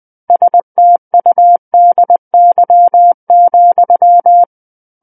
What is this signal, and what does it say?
Morse code, heard as a keyed tone, about 20 wpm, 700 Hz. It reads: STUDY,
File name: 20 WPM